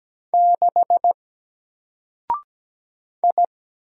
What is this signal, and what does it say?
Morse code, heard as a keyed tone, about 17 wpm, 700 Hz. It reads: 6 I
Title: Single Letter-Number 17wpm